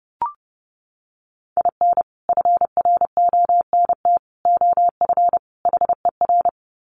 Morse code, heard as a keyed tone, about 30 wpm, 700 Hz. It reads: IN FRONT OF HER
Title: Sets of 4 Words 30wpm